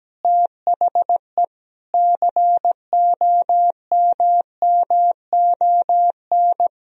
Morse code, heard as a keyed tone, about 17 wpm, 700 Hz. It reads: THE COMMON
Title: Sentences from Top 1000 Words - Review Part 3 - 17wpm